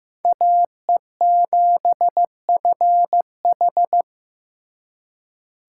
Morse code, heard as a keyed tone, about 18 wpm, 700 Hz. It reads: AE7FH
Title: Speed Racing - USA 2x2 Call Signs 15wpm